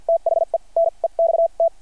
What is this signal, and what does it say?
Morse code, heard as a keyed tone, about 49 wpm, spaced as for 39 wpm, 660 Hz. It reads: THE NEXT
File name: archived / CRSnet